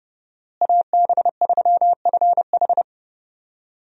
Morse code, heard as a keyed tone, about 31 wpm, 700 Hz. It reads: AB3FH
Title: USA Call Signs - 5 Characters - Repeated 2x - 30wpm